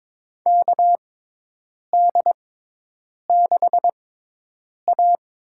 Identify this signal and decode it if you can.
Morse code, heard as a keyed tone, about 22 wpm, spaced as for 8 wpm, 700 Hz. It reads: KD6A
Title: USA Call Signs - 6 Characters - 6x Character Spacing - 22wpm